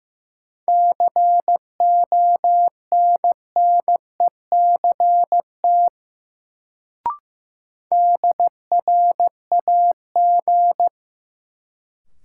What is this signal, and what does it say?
Morse code, heard as a keyed tone, about 15 wpm, 700 Hz. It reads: CONNECT DRAG
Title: Sentences from Top 1500 Words - New Word Drill - 15wpm